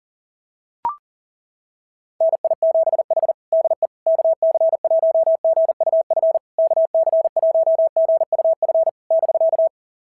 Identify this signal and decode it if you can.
Morse code, heard as a keyed tone, about 40 wpm, 690 Hz. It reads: NI7H DE KC1GUF KC1GUF <BK>
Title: SOTA QSO Practice 40wpm